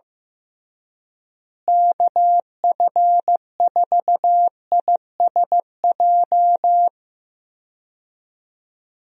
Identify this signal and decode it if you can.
Morse code, heard as a keyed tone, about 15 wpm, 700 Hz. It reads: KF4ISJ
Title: USA Call Signs - 6 Characters - Repeated 3x - 15wpm